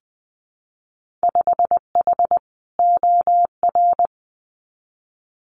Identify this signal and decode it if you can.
Morse code, heard as a keyed tone, about 20 wpm, 700 Hz. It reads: 5H OR
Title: Winter Field Day Exchanges Encore 20wpm